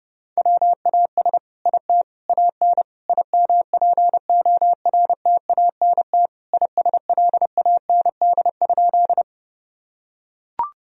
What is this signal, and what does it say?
Morse code, heard as a keyed tone, about 30 wpm, 700 Hz. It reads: WAS IT AN IMPORTANT ISLAND?
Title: Sentences from Top 1000 Words - Review Part 3 - 30wpm